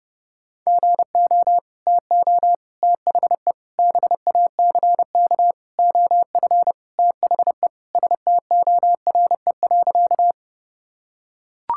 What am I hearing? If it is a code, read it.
Morse code, heard as a keyed tone, about 30 wpm, 700 Hz. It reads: GO TO THE BACK OF THE STORE.
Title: Sentences from Top 1000 Words - Review Part 5 - 30wpm